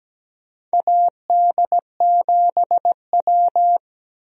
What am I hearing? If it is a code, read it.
Morse code, heard as a keyed tone, about 17 wpm, 700 Hz. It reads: AD7W
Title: USA Call Signs - 4 Characters - Repeated 2x - 17wpm